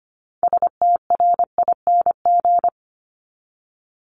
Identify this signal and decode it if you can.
Morse code, heard as a keyed tone, about 25 wpm, 700 Hz. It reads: STRING
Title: Top 1501-1750 Words 25wpm